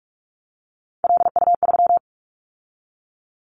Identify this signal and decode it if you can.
Morse code, heard as a keyed tone, about 46 wpm, 700 Hz. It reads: LU3
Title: Callsign Prefixes for Top 100 DXCC - 45wpm